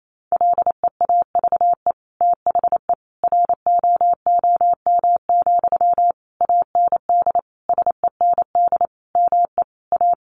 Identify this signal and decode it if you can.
Morse code, heard as a keyed tone, about 28 wpm, 700 Hz. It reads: LEAVE THE ROOM, AND SEND ME A
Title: Sentences from Top 1500 Words 28wpm